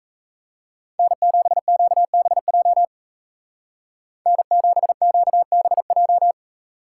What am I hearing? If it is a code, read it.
Morse code, heard as a keyed tone, about 34 wpm, 700 Hz. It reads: N7QBJ N7QBJ
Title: Speed Racing - USA Call Signs 28wpm